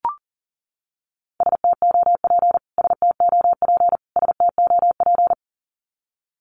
Morse code, heard as a keyed tone, about 40 wpm, 700 Hz. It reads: STOP STOP STOP
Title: Top 300 Words Repeated 40wpm